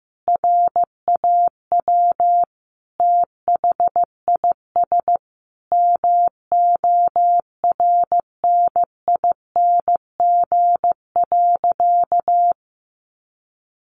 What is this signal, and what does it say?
Morse code, heard as a keyed tone, about 15 wpm, 700 Hz. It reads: RAW THIS MORNING.
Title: Sentences from Top 1500 Words 15wpm